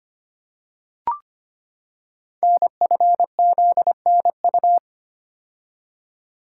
Morse code, heard as a keyed tone, about 25 wpm, 700 Hz. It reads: NFZNU
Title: Code Groups - 5 Letters - 25wpm